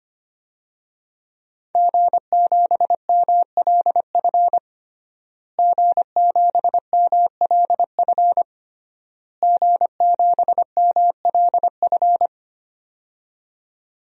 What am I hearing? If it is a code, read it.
Morse code, heard as a keyed tone, about 25 wpm, 700 Hz. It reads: G7MLF G7MLF G7MLF
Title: DXCC-100 Call Signs - 5 Characters - Repeated 3x - 25wpm